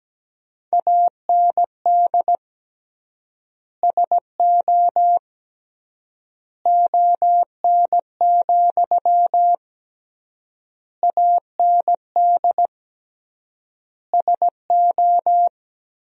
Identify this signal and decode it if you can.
Morse code, heard as a keyed tone, about 17 wpm, 700 Hz. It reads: AND SO ON, AND SO
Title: 3x Word-Spacing - Sentences from Top 100 Words - 17wpm